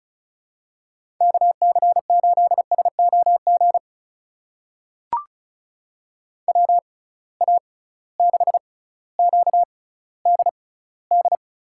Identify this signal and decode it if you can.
Morse code, heard as a keyed tone, about 34 wpm, spaced as for 18 wpm, 700 Hz. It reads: KC8SOG WA6QDD
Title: USA Call Signs - 6 Characters - 6x Character Spacing - 35wpm